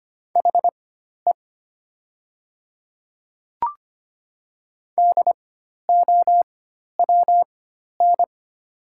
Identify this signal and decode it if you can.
Morse code, heard as a keyed tone, about 25 wpm, spaced as for 12 wpm, 700 Hz. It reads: HE DOWN
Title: 4x Character Spacing - Top 100 Words - 25wpm